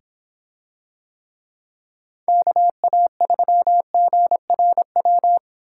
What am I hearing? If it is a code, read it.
Morse code, heard as a keyed tone, about 26 wpm, 700 Hz. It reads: KA3GRW